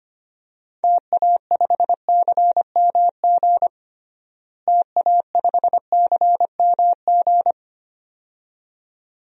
Morse code, heard as a keyed tone, about 25 wpm, 700 Hz. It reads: TA5CMG TA5CMG
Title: DXCC-100 Call Signs - 6 Characters - Repeated 2x - 25wpm